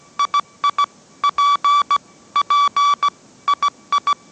Morse code, heard as a keyed tone, about 17 wpm, 1150 Hz. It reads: II P P II
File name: mrx / mrx080 18wpm